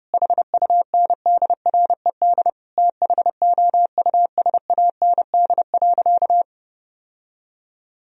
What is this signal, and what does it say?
Morse code, heard as a keyed tone, about 30 wpm, 700 Hz. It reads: HUNDRED THOUSAND.
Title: Sentences from Top 1000 Words - Review Part 3 - 30wpm